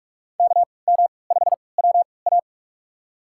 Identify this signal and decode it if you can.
Morse code, heard as a keyed tone, about 46 wpm, spaced as for 25 wpm, 700 Hz. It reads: KM5WA
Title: USA Call Signs - 5 Characters - 3x Character Spacing - 45wpm